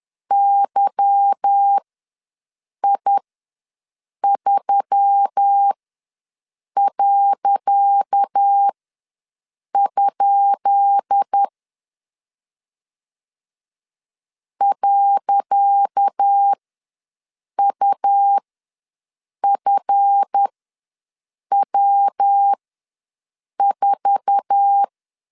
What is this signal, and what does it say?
Morse code, heard as a keyed tone, about 10 wpm, spaced as for 6 wpm, 800 Hz. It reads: YI3.? .UFW4
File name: LEZIONI / 10 SIGNS / 10C DOT ws 10wpm 4213